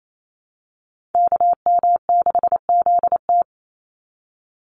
Morse code, heard as a keyed tone, about 28 wpm, 700 Hz. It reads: KM6ZT
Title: Call Signs Repeated 2x - 28wpm